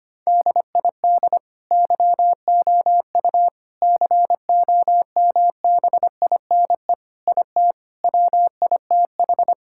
Morse code, heard as a keyed tone, about 25 wpm, 700 Hz. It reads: DID YOU COMBINE IT WITH
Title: Sentences from Top 1600 Words 25wpm